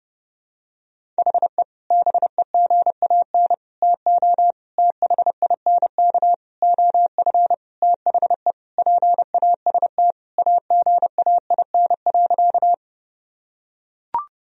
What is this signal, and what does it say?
Morse code, heard as a keyed tone, about 32 wpm, 700 Hz. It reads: HE BEGAN TO THINK OF THE PAST AGAIN.
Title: Sentences from Top 1000 Words - Review Part 3 - 30wpm